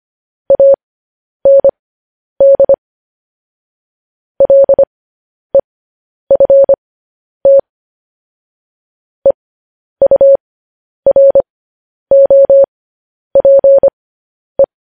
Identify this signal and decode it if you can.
Morse code, heard as a keyed tone, about 25 wpm, spaced as for 10 wpm, 550 Hz. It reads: AND LEFT EUROPE
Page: news-headlines-in-morse-code-at-10-wpm